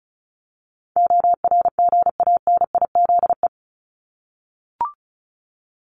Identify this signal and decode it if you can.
Morse code, heard as a keyed tone, about 34 wpm, 700 Hz. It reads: ORGANIZE
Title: Top 1751-2000 Words 35wpm